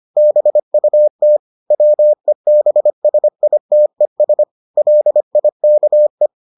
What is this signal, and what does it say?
Morse code, heard as a keyed tone, about 25 wpm, 600 Hz. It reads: BUT WEBSITES LIKE